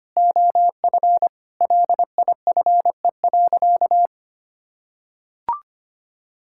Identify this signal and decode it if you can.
Morse code, heard as a keyed tone, about 25 wpm, 700 Hz. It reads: OF LIFE.
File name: practice / 25wpm / Sentences with 3-Word Phrases - 25wpm